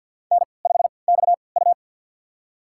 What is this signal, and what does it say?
Morse code, heard as a keyed tone, about 52 wpm, spaced as for 29 wpm, 700 Hz. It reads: N5XU
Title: USA Call Signs - 4 Characters - 3x Character Spacing - 50wpm